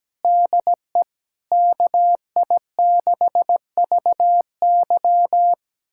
Morse code, heard as a keyed tone, about 17 wpm, 700 Hz. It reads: DE KI6VY
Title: USA Callsign DE Callsign 17wpm